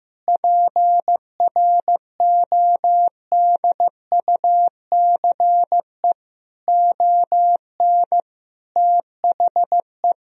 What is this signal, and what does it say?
Morse code, heard as a keyed tone, about 15 wpm, 700 Hz. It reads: PRODUCE ON THE
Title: Sentences from Top 1000 Words - Review Part 3 - 15wpm